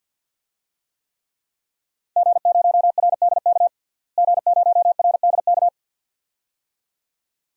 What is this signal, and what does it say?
Morse code, heard as a keyed tone, about 52 wpm, 700 Hz. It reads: G0RDK G0RDK
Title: DXCC-100 Call Signs - 5 Characters - Repeated 2x - 50wpm